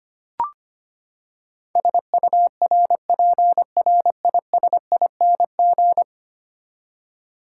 Morse code, heard as a keyed tone, about 25 wpm, 700 Hz. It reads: SURPRISING
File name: Ten Letter Words - 25wpm